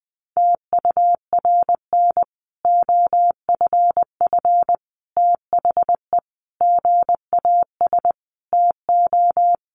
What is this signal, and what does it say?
Morse code, heard as a keyed tone, about 20 wpm, 700 Hz. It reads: TURN OFF THE GAS TO
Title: Sentences from Top 1100 Words 20wpm